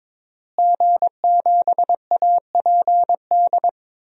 Morse code, heard as a keyed tone, about 22 wpm, 700 Hz. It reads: G7APD
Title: DXCC-100 Call Signs - 5 Characters - Repeated 3x - 22wpm